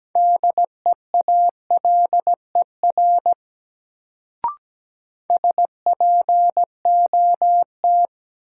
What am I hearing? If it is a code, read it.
Morse code, heard as a keyed tone, about 17 wpm, 700 Hz. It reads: DEALER SPOT
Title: Sentences from Top 1100 Words - New Word Drill Alt - 17wpm